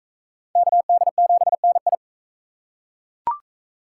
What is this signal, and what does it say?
Morse code, heard as a keyed tone, about 41 wpm, 700 Hz. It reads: KD7NI